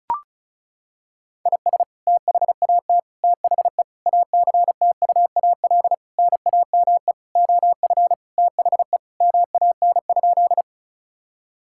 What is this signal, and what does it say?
Morse code, heard as a keyed tone, about 36 wpm, 700 Hz. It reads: IS THAT THE ACTUAL NAME OF THE MAN?
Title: Sentences from Top 1000 Words - Review Part 5 - 35wpm